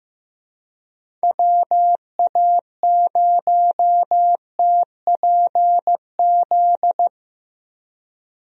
Morse code, heard as a keyed tone, about 15 wpm, 700 Hz. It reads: WA0TPZ
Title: USA Call Signs - 6 Characters - Repeated 2x - 15wpm